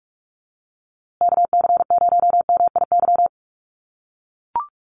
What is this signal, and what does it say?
Morse code, heard as a keyed tone, about 44 wpm, 700 Hz. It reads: KC0MIY
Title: USA Call Signs Repeated 3x 45wpm